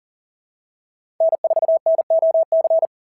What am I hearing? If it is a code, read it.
Morse code, heard as a keyed tone, about 40 wpm, 650 Hz. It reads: N4NOC
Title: SOTA QSO Practice 40wpm